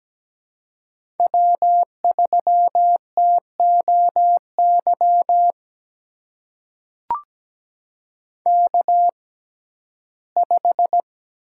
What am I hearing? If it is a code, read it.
Morse code, heard as a keyed tone, about 17 wpm, spaced as for 12 wpm, 700 Hz. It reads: W3TOY K5
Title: USA Call Signs - 6x Character Spacing - 17wpm